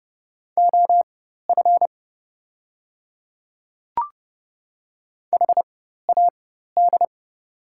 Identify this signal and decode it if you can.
Morse code, heard as a keyed tone, about 30 wpm, spaced as for 14 wpm, 700 Hz. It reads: OF HAD
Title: 4x Character Spacing - Top 200 Words - 30wpm